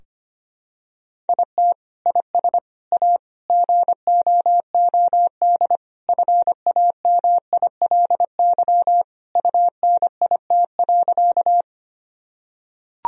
Morse code, heard as a keyed tone, about 25 wpm, 700 Hz. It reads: IT IS A GOOD FAMILY UNIT.
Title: Sentences from Top 1000 Words - Review Part 2 - 25wpm